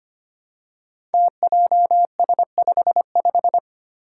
Morse code, heard as a keyed tone, about 25 wpm, 700 Hz. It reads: TJS55
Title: Five Challenging Letter-Number Combinations 25wpm